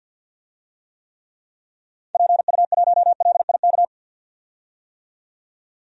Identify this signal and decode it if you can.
Morse code, heard as a keyed tone, about 51 wpm, 700 Hz. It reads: PU1LIK
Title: DXCC-100 Call Signs - 6 Characters - 50wpm